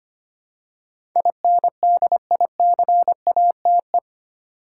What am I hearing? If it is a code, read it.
Morse code, heard as a keyed tone, about 25 wpm, 700 Hz. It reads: INDICATE
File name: Eight Letter Words - 25wpm